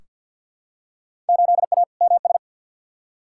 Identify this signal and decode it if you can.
Morse code, heard as a keyed tone, about 47 wpm, 700 Hz. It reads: 8A MS
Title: Field Day Exchanges - 50wpm